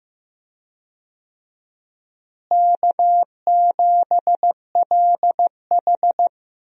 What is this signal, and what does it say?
Morse code, heard as a keyed tone, about 15 wpm, 700 Hz. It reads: K7LH
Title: USA Call Signs - 4 Characters - Repeated 2x - 15wpm